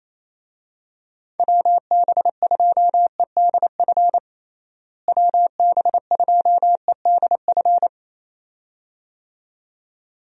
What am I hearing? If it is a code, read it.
Morse code, heard as a keyed tone, about 28 wpm, 700 Hz. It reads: WB2EDF WB2EDF
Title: USA Call Signs - 6 Characters - Repeated 2x - 28wpm